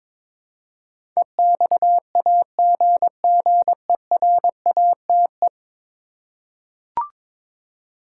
Morse code, heard as a keyed tone, about 22 wpm, 700 Hz. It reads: EXAGGERATE
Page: morse-code-ninja-practice-22wpm